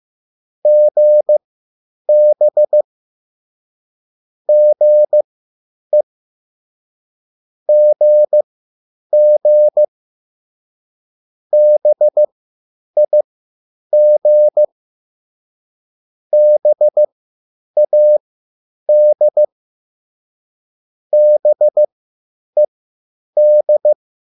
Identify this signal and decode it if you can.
Morse code, heard as a keyed tone, about 15 wpm, spaced as for 9 wpm, 600 Hz. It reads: GB GE GG BIG BAD BED
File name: QSL / CW lesson